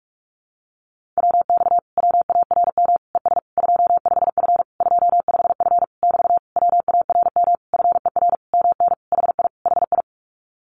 Morse code, heard as a keyed tone, about 44 wpm, 700 Hz. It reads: WX WARM ES 25F 25F <BT> WARM FER MN HI HI
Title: Common QSO Phrases Encore 45wpm